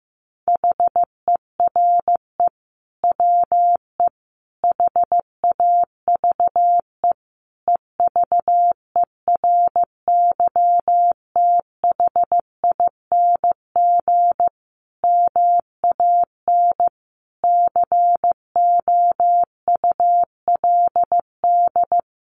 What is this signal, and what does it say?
Morse code, heard as a keyed tone, about 15 wpm, 700 Hz. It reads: HERE WE HAVE EVERYTHING MAN COULD
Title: Sentences from Top 1100 Words 15wpm